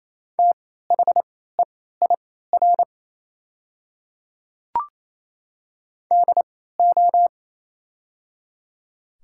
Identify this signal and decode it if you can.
Morse code, heard as a keyed tone, about 28 wpm, spaced as for 16 wpm, 690 Hz. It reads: THEIR DO
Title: 3x Character Spacing - Top 100 Words - 28wpm